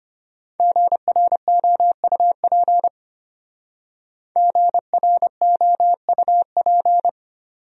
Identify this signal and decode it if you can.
Morse code, heard as a keyed tone, about 31 wpm, 700 Hz. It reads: GROUP GROUP
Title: Speed-Racing Top 500 Words 25wpm